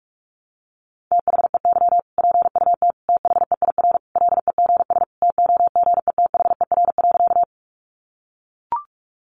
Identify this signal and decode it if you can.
Morse code, heard as a keyed tone, about 44 wpm, 700 Hz. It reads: THEY PUT THEIR LEGS TOGETHER.
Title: Easy Phrases 45wpm